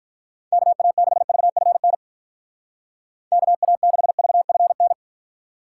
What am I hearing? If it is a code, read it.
Morse code, heard as a keyed tone, about 45 wpm, 700 Hz. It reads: KA6VFN KA6VFN